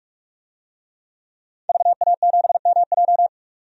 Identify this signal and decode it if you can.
Morse code, heard as a keyed tone, about 44 wpm, 700 Hz. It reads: VA7MJ
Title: DXCC-100 Call Signs - 5 Characters - 45wpm